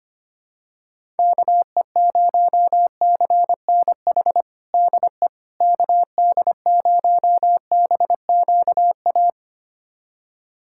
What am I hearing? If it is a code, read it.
Morse code, heard as a keyed tone, about 25 wpm, 700 Hz. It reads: KE0CNH DE KD0BQA
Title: USA Callsign DE Callsign - 6 Characters - 25wpm